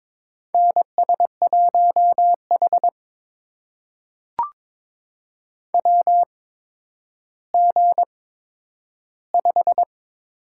Morse code, heard as a keyed tone, about 22 wpm, spaced as for 11 wpm, 700 Hz. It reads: NS1H WG5
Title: USA Call Signs - 4 Characters - 8x Character Spacing - 22wpm